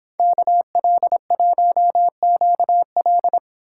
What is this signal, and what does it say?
Morse code, heard as a keyed tone, about 29 wpm, 700 Hz. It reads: KL1QL